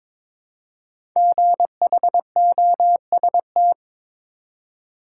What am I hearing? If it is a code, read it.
Morse code, heard as a keyed tone, about 22 wpm, 700 Hz. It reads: GHOST
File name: 5 Letter Words - First 3 Letters Given - 22wpm